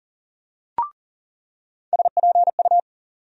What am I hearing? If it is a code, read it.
Morse code, heard as a keyed tone, about 37 wpm, 700 Hz. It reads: SPU
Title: USA Call Sign Postfixes - 3 Letters 40wpm